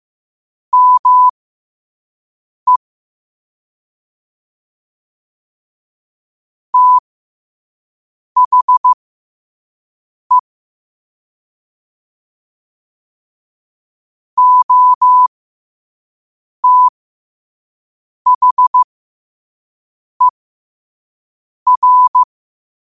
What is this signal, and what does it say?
Morse code, heard as a keyed tone, about 15 wpm, spaced as for 5 wpm, 1000 Hz. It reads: ME THE OTHER
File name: qotdfiles / 5wpm